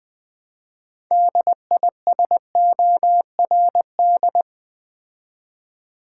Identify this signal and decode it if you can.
Morse code, heard as a keyed tone, about 20 wpm, 700 Hz. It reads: DISORD
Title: Build A Word - 3 Letters At A Time - 8-Letter Words - 20wpm